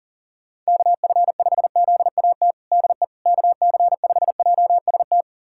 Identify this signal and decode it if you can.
Morse code, heard as a keyed tone, about 40 wpm, 700 Hz. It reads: KF5ZAT DE KC5JST
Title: USA Callsign DE Callsign - 6 Characters - 40wpm